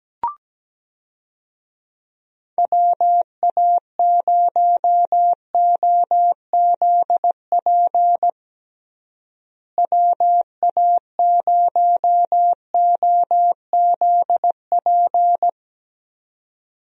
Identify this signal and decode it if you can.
Morse code, heard as a keyed tone, about 17 wpm, 700 Hz. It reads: WA0OZP WA0OZP
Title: USA Call Signs - 6 Characters - Repeated 2x - 17wpm